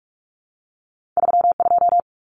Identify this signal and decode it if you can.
Morse code, heard as a keyed tone, about 44 wpm, 700 Hz. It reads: 32
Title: Common Cut Numbers - 2-Digit 45wpm